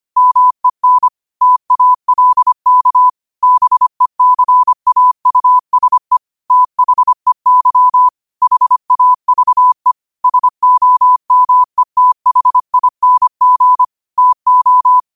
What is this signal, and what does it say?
Morse code, heard as a keyed tone, about 25 wpm, 1000 Hz. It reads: MEN TALK BECAUSE THEY HAVE SOMETHING TO
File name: qotdfiles / 25wpm